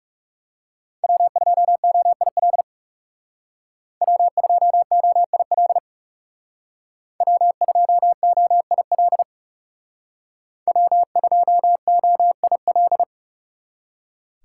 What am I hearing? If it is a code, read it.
Morse code, heard as a keyed tone, about 36 wpm, 700 Hz. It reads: W2OIL W2OIL W2OIL W2OIL
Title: Speed Racing - USA 1x3 Call Signs 30wpm